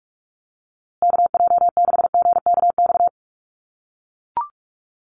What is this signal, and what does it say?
Morse code, heard as a keyed tone, about 44 wpm, 700 Hz. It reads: KJ6GKX
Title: USA Call Signs Repeated 3x 45wpm